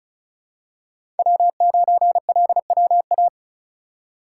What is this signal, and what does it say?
Morse code, heard as a keyed tone, about 36 wpm, 700 Hz. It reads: W9LWA
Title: USA Call Signs - 4x Character Spacing - 35wpm